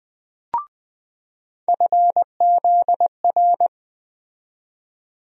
Morse code, heard as a keyed tone, about 20 wpm, 700 Hz. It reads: FZR
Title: USA Call Sign Postfixes - 3 Letters 20wpm